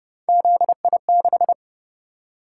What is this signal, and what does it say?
Morse code, heard as a keyed tone, about 29 wpm, 700 Hz. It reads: ZI6